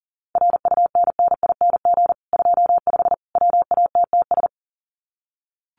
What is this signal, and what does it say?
Morse code, heard as a keyed tone, about 37 wpm, 700 Hz. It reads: RUNNING 25 WATTS
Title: Common QSO Phrases 40wpm